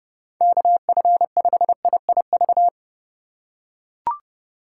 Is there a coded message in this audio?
Yes: Morse code from 0.4 to 2.7 s, heard as a keyed tone, about 29 wpm, 700 Hz. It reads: KF5IIV